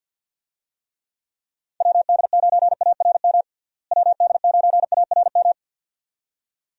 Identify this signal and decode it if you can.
Morse code, heard as a keyed tone, about 50 wpm, 700 Hz. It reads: WD9ARM WD9ARM
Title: USA Call Signs - 6 Characters - Repeated 2x - 50wpm